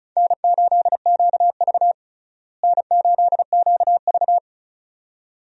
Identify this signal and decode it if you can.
Morse code, heard as a keyed tone, about 36 wpm, 700 Hz. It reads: N8QV N8QV
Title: USA Call Signs - 4 Characters - Repeated 2x - 35wpm